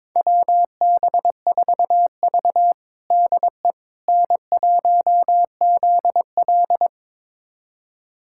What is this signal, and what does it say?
Morse code, heard as a keyed tone, about 22 wpm, 700 Hz. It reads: WB4V DE N1ZL
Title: USA Callsign DE Callsign - 4 Characters - 22wpm